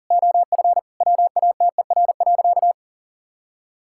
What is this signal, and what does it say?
Morse code, heard as a keyed tone, about 40 wpm, 700 Hz. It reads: OF WATER.
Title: Sentences from Top 1000 Words - Review Part 5 - 40wpm